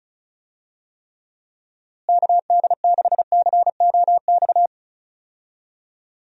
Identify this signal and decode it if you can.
Morse code, heard as a keyed tone, about 36 wpm, 700 Hz. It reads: KD6COX
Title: USA Call Signs - 6 Characters - Repeated 2x - 35wpm